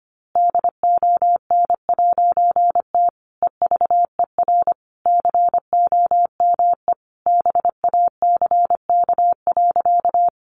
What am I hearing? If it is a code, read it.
Morse code, heard as a keyed tone, about 25 wpm, 700 Hz. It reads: DON'T EVER COME BACK.
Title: A Sound Of Thunder 25wpm